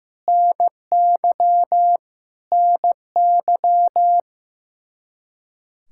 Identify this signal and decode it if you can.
Morse code, heard as a keyed tone, about 15 wpm, 700 Hz. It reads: NY NY
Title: Practice QSO - Exchange 1 - Beginning - 15wpm